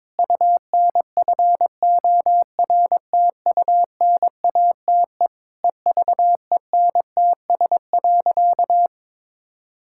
Morse code, heard as a keyed tone, about 22 wpm, 700 Hz. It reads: UNFORTUNATE EVENTS.
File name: practice / 22wpm / Sentences with 8-Word Phrases - 22wpm